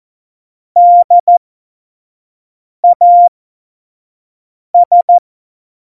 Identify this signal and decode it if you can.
Morse code, heard as a keyed tone, about 14 wpm, spaced as for 5 wpm, 700 Hz. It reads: DAS
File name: Morsekurs / Witze / Witz 20 March 2018, 25 F70